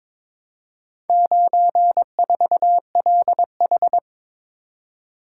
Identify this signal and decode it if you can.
Morse code, heard as a keyed tone, about 22 wpm, 700 Hz. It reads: 94LH
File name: practice / 22wpm / Four Letter-Number Combinations - 22wpm